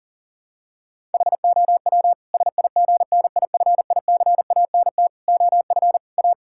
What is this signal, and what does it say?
Morse code, heard as a keyed tone, about 40 wpm, 700 Hz. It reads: HOW SIGNIFICANT OF A
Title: Sentences from Top 1000 Words - Review Part 6 - 40wpm